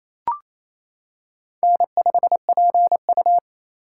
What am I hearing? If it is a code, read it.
Morse code, heard as a keyed tone, about 27 wpm, 700 Hz. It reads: N5PU
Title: K1USN SST QSO Practice - 28wpm